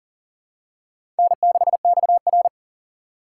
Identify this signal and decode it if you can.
Morse code, heard as a keyed tone, about 38 wpm, 690 Hz. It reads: N6XR
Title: USA Call Signs - 4 Characters - 2x Character Spacing - 40wpm